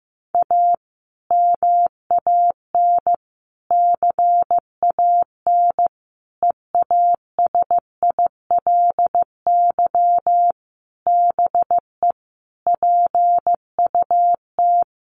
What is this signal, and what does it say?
Morse code, heard as a keyed tone, about 14 wpm, 700 Hz. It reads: A MAN CAN EASILY BE PUT
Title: Sentences from Top 900 Words 15wpm